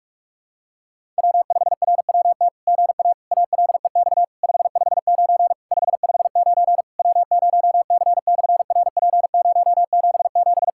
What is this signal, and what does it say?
Morse code, heard as a keyed tone, about 44 wpm, 700 Hz. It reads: W5RWT GA ALEX 559 559 W0C/RP077
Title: SOTA QSO Practice 45wpm